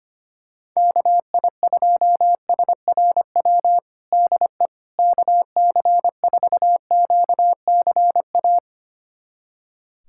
Morse code, heard as a keyed tone, about 25 wpm, 700 Hz. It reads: KI2SRW DE KC4QCA
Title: USA Callsign DE Callsign 25wpm